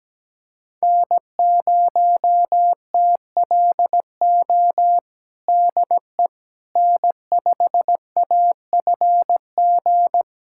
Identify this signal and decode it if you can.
Morse code, heard as a keyed tone, about 17 wpm, 700 Hz. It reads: N0TLO DE N5AFG
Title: USA Callsign DE Callsign - 5 Characters - 17wpm